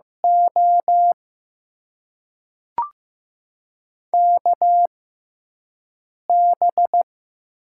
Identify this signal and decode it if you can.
Morse code, heard as a keyed tone, about 15 wpm, spaced as for 4 wpm, 700 Hz. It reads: OKB
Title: USA Call Signs - 6x Character Spacing - 15wpm